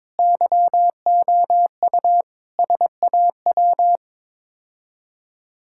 Morse code, heard as a keyed tone, about 21 wpm, 700 Hz. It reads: YOU SAW
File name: practice / 22wpm / Sets of 7 Words - 22wpm